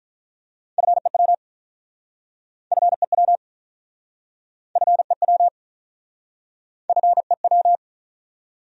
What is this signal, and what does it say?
Morse code, heard as a keyed tone, about 47 wpm, 700 Hz. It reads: FEW FEW FEW FEW